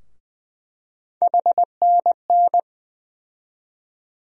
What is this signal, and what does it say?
Morse code, heard as a keyed tone, about 21 wpm, 700 Hz. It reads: HNN
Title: USA Call Sign Postfixes - 3 Letters 20wpm